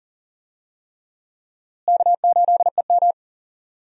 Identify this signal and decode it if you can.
Morse code, heard as a keyed tone, about 40 wpm, 700 Hz. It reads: K8EM
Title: USA Call Signs - 4 Characters - Repeated 3x - 40wpm